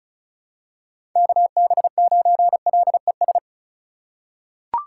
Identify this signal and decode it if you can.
Morse code, heard as a keyed tone, about 34 wpm, 700 Hz. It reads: KB9LES
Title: USA Call Signs - 3x Character Spacing - 35wpm